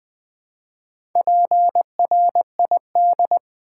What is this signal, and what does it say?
Morse code, heard as a keyed tone, about 20 wpm, 700 Hz. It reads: PRID
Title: Build A Word - 1 Letter At A Time - 5-Letter Words - 20wpm